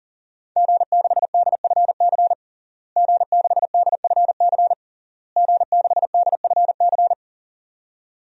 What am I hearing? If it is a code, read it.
Morse code, heard as a keyed tone, about 40 wpm, 700 Hz. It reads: G6DFC G6DFC G6DFC
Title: DXCC-100 Call Signs - 5 Characters - Repeated 3x - 40wpm